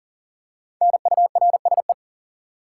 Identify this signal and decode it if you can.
Morse code, heard as a keyed tone, about 40 wpm, 700 Hz. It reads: NURSE
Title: Five Letter Words - 40wpm